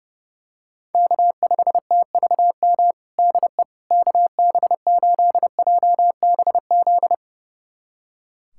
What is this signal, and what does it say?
Morse code, heard as a keyed tone, about 30 wpm, 700 Hz. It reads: K5TVM DE KB8JBZ
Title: USA Callsign DE Callsign 30wpm